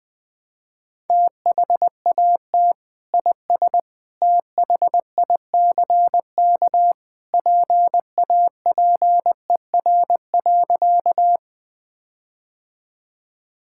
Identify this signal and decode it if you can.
Morse code, heard as a keyed tone, about 20 wpm, 700 Hz. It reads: THAT IS THICK PAPER.
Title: Sentences from Top 1600 Words 20wpm